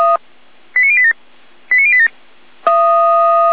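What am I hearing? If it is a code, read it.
Five-tone selective call (ZVEI), heard as quick runs of five tones, tones of about 70 ms. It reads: 89087 89087
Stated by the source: ZVEI five-tone call received by SDR.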